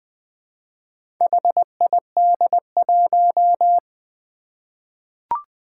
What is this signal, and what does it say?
Morse code, heard as a keyed tone, about 20 wpm, 700 Hz. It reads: HID1
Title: Four Challenging Letter-Number Combinations 20wpm